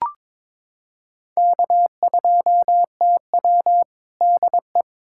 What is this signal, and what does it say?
Morse code, heard as a keyed tone, about 22 wpm, 700 Hz. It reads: K2TW DE
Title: K1USN SST QSO Practice 22wpm